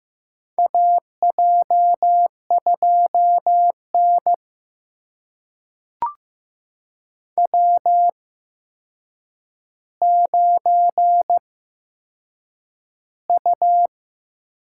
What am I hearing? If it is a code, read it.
Morse code, heard as a keyed tone, about 15 wpm, spaced as for 7 wpm, 700 Hz. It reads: AJ2N W9U
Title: USA Call Signs - 4 Characters - 8x Character Spacing - 15wpm